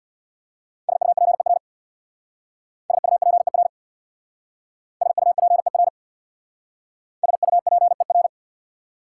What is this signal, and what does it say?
Morse code, heard as a keyed tone, about 69 wpm, 690 Hz. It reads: SUPER SUPER SUPER SUPER